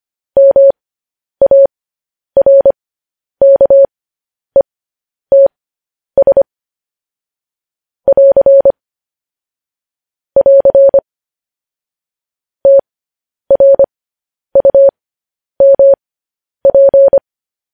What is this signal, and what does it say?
Morse code, heard as a keyed tone, about 25 wpm, spaced as for 10 wpm, 550 Hz. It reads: MARKETS <AR> <AR> TRUMP
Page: News Headlines in Morse Code at 10 WPM